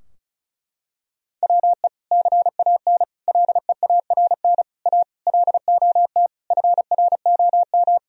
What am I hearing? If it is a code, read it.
Morse code, heard as a keyed tone, about 34 wpm, 700 Hz. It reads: WE CAN LEARN A LOT FROM
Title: Sentences from Top 1000 Words - Review Part 2 - 35wpm